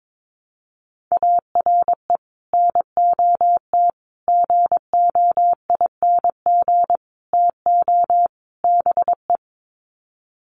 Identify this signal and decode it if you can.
Morse code, heard as a keyed tone, about 22 wpm, 700 Hz. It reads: ARE NOT GOING TO BE
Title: Sets of 5 Words Encore 22wpm